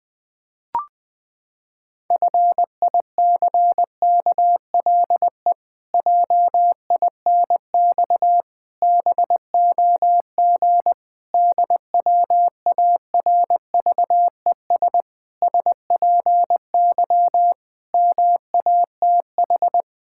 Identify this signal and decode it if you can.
Morse code, heard as a keyed tone, about 20 wpm, 700 Hz. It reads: FICKLE JINX BOG DWARVES SPY MATH
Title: Pangram Challenge v2 - 20wpm